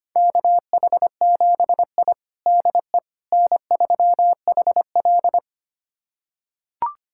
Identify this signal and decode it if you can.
Morse code, heard as a keyed tone, about 25 wpm, 700 Hz. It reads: KH7I DE N3HL
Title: USA Callsign DE Callsign - 4 Characters - 25wpm